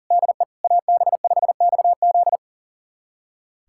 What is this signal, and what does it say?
Morse code, heard as a keyed tone, about 37 wpm, 700 Hz. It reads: DE AB5XZ
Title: USA Callsign DE Callsign 40wpm